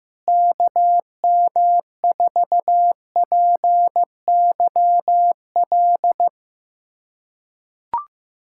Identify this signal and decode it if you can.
Morse code, heard as a keyed tone, about 15 wpm, 700 Hz. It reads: KM4PYL
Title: USA Callsign DE Callsign 15wpm